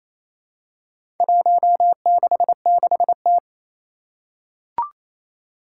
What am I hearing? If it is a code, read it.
Morse code, heard as a keyed tone, about 27 wpm, 700 Hz. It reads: 166T
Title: Four Challenging Letter-Number Combinations 28wpm